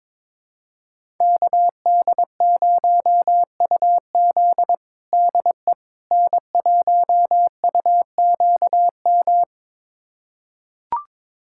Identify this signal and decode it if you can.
Morse code, heard as a keyed tone, about 21 wpm, 700 Hz. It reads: KD0UZ DE N1UQM
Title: USA Callsign DE Callsign - 5 Characters - 22wpm